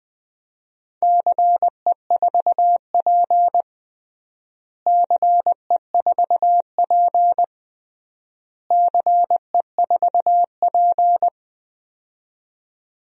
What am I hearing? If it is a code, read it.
Morse code, heard as a keyed tone, about 20 wpm, 700 Hz. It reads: CE4P CE4P CE4P
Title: DXCC-100 Call Signs - 4 Characters - Repeated 3x - 20wpm